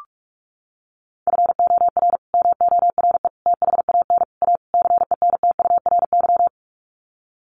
Morse code, heard as a keyed tone, about 44 wpm, 700 Hz. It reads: FOR MORE THAN A CENTURY
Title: Sets of 5 Words Encore2 45wpm